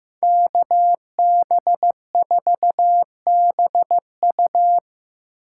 Morse code, heard as a keyed tone, about 18 wpm, 700 Hz. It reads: KB4BU
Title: Speed Racing - USA 2x2 Call Signs 15wpm